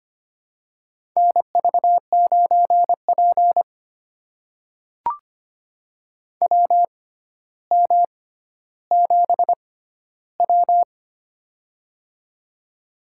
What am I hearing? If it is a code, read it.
Morse code, heard as a keyed tone, about 25 wpm, spaced as for 13 wpm, 700 Hz. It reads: NV9P WM7W
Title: USA Call Signs - 4 Characters - 6x Character Spacing - 25wpm